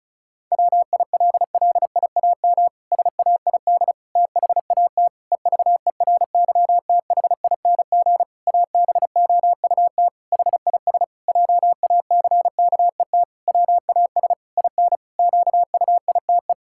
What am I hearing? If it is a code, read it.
Morse code, heard as a keyed tone, about 34 wpm, 700 Hz. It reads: WILLIAM SAID THAT EVERYTHING ABOUT HIS JACKET WAS IN QUITE
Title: Pangram Challenge v2 - 35wpm